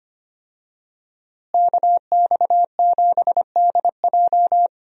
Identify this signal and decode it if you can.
Morse code, heard as a keyed tone, about 25 wpm, 700 Hz. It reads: KX7DJ
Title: USA Call Signs - 5 Characters - Repeated 3x - 25wpm